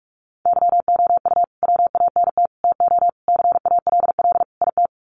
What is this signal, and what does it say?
Morse code, heard as a keyed tone, about 44 wpm, 700 Hz. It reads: YOU WANT TO CALL IT
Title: Sets of 5 Words Encore2 45wpm